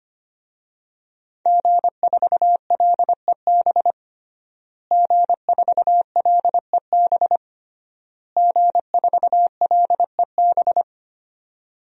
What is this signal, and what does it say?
Morse code, heard as a keyed tone, about 25 wpm, 700 Hz. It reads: G4LEB G4LEB G4LEB
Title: DXCC-100 Call Signs - 5 Characters - Repeated 3x - 25wpm